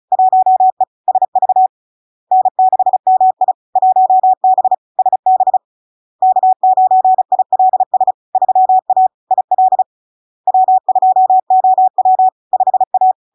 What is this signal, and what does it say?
Morse code, heard as a keyed tone, about 36 wpm, 750 Hz. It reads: 1E SV N6MI 1B SB K9ILS 3A IL W2OW 5A
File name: LARK-FD-2022-Log-in-CW-35wpm-750-Hz